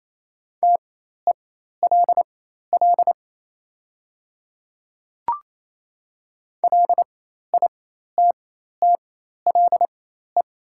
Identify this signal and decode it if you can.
Morse code, heard as a keyed tone, about 28 wpm, spaced as for 13 wpm, 700 Hz. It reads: TELL LITTLE
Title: 4x Character Spacing - Top 300 Words - 28wpm